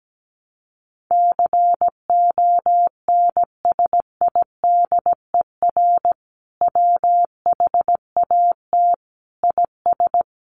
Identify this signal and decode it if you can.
Morse code, heard as a keyed tone, about 17 wpm, 700 Hz. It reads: CONSIDER WHAT IS
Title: Sentences from Top 800 Words 17wpm